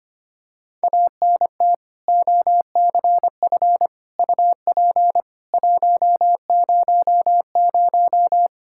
Morse code, heard as a keyed tone, about 25 wpm, 700 Hz. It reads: ANT OCF UP 100
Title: Practice QSO - Exchange 2 - Info - 25wpm